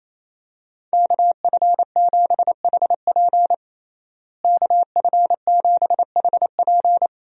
Morse code, heard as a keyed tone, about 28 wpm, 700 Hz. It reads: KF7HP KF7HP
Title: USA Call Signs - 5 Characters - Repeated 3x - 28wpm